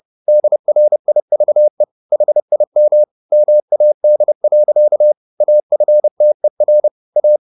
Morse code, heard as a keyed tone, about 30 wpm, 600 Hz. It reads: DRIVE HIM MAD. AFTER A